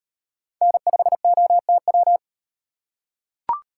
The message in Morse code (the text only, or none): N5OTW